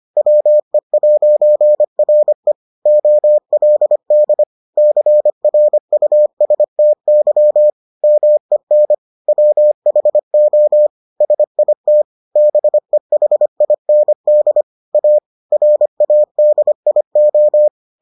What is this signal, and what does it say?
Morse code, heard as a keyed tone, about 25 wpm, 600 Hz. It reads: WE'RE OLD CRUSTY MEN WHO SIT BEHIND A RADIO